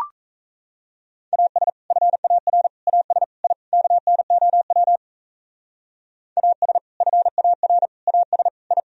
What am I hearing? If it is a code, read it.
Morse code, heard as a keyed tone, about 36 wpm, 700 Hz. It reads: AS FAR AS I KNOW AS FAR AS I